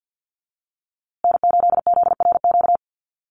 Morse code, heard as a keyed tone, about 50 wpm, 700 Hz. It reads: N8ZRQ
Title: USA Call Signs Repeated 3x 50wpm